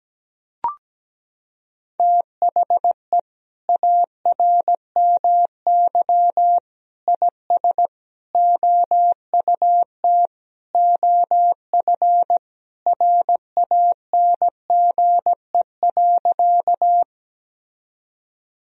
Morse code, heard as a keyed tone, about 17 wpm, 700 Hz. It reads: THE ARMY IS OUT OF RANGE.
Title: Sentences from Top 1000 Words - Review Part 4 - 17wpm